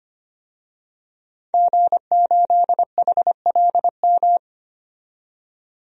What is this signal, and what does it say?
Morse code, heard as a keyed tone, about 25 wpm, 700 Hz. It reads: G8HLM
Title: DXCC-100 Call Signs - 5 Characters - 25wpm